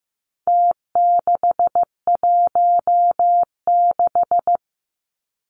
Morse code, heard as a keyed tone, about 15 wpm, 700 Hz. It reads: T616
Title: Numbers - Contest Style - 4-Digit 15wpm